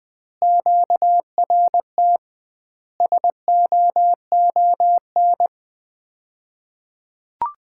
Morse code, heard as a keyed tone, about 20 wpm, 700 Hz. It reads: QRT SOON
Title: Common QSO Phrases - 2x Word-Spacing - 20wpm